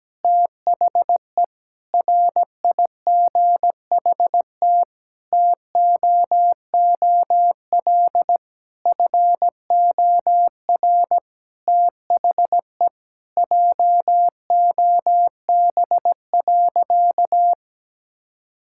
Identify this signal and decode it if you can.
Morse code, heard as a keyed tone, about 17 wpm, 700 Hz. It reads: THE RIGHT TOOL FOR THE JOB.
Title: Sentences from Top 1000 Words - Review Part 6 - 17wpm